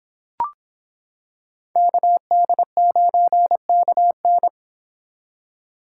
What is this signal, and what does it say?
Morse code, heard as a keyed tone, about 26 wpm, 700 Hz. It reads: KD9KN